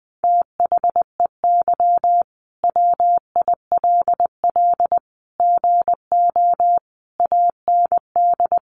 Morse code, heard as a keyed tone, about 19 wpm, 700 Hz. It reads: THEY WILL GO AND
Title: Sentences from Top 1100 Words 20wpm